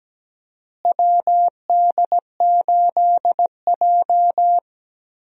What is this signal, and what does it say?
Morse code, heard as a keyed tone, about 17 wpm, 700 Hz. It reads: WD8J
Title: USA Call Signs - 4 Characters - Repeated 3x - 17wpm